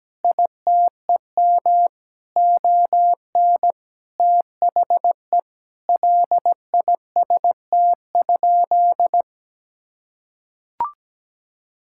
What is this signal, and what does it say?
Morse code, heard as a keyed tone, about 17 wpm, 700 Hz. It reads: ITEM ON THE LIST?
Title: Sentences from Top 1000 Words - Review Part 6 - 17wpm